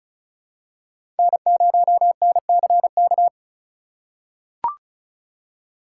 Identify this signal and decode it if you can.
Morse code, heard as a keyed tone, about 36 wpm, 700 Hz. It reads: N0NCK
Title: USA Call Signs - 5 Characters - 4x Character Spacing - 35wpm